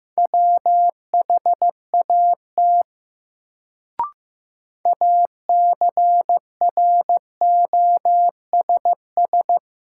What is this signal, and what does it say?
Morse code, heard as a keyed tone, about 15 wpm, 700 Hz. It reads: WHAT ACROSS
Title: Sets of 2 Words - New Word Drill - 15wpm